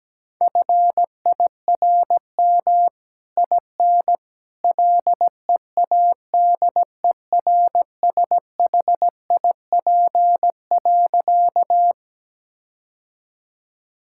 Morse code, heard as a keyed tone, about 17 wpm, 700 Hz. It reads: FIRM IN LEADERSHIP.
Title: Sentences from Top 1600 Words 17wpm